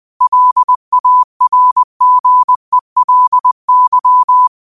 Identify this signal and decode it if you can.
Morse code, heard as a keyed tone, about 20 wpm, 1000 Hz. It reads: LARGELY